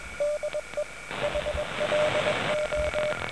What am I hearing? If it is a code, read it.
Morse code, heard as a keyed tone, about 21 wpm, 590 Hz, with noise. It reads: DE HLO